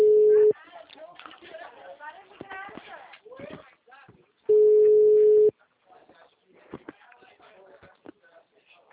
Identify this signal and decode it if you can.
Ringback tone (Europe, 425 Hz), with speech.